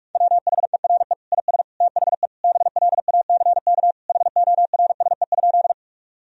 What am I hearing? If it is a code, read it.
Morse code, heard as a keyed tone, about 41 wpm, 700 Hz. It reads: WHERE IS THE BLACK HORSE?
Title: Sentences from Top 1000 Words - Review Part 2 - 45wpm